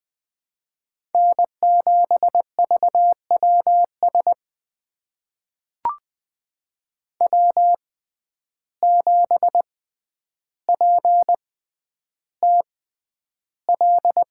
Morse code, heard as a keyed tone, about 20 wpm, spaced as for 10 wpm, 700 Hz. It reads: N7VWS W7PTL
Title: USA Call Signs - 5 Characters - 6x Character Spacing - 20wpm